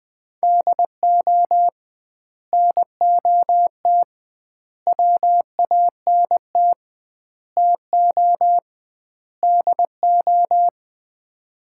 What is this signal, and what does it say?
Morse code, heard as a keyed tone, about 20 wpm, 700 Hz. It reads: DO NOT WANT TO DO
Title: 2x Word-Spacing - Sets of 5 Words - 20wpm